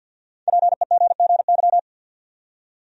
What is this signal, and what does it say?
Morse code, heard as a keyed tone, about 52 wpm, 700 Hz. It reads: PEGGY
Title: USA Common Names - Max 5 Letters - 50wpm